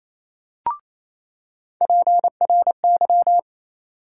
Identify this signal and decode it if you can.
Morse code, heard as a keyed tone, about 28 wpm, 700 Hz. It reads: PRY
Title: USA Call Sign Postfixes - 3 Letters 28wpm